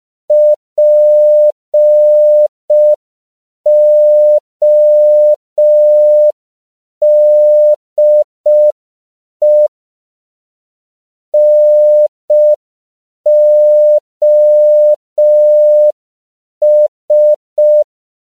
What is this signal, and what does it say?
Morse code, heard as a keyed tone, about 5 wpm, 600 Hz. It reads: PODE NOS